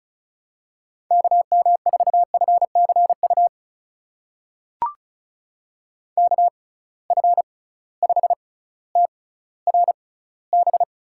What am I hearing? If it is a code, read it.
Morse code, heard as a keyed tone, about 34 wpm, spaced as for 18 wpm, 700 Hz. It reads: KM4FCU KF5TRB
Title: USA Call Signs - 6 Characters - 6x Character Spacing - 35wpm